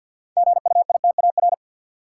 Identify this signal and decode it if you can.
Morse code, heard as a keyed tone, about 51 wpm, 700 Hz. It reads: GUITAR
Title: Six Letter Words - 50wpm